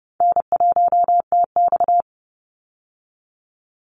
Morse code, heard as a keyed tone, about 30 wpm, 700 Hz. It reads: N1TX
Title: Easy Call Signs - 30wpm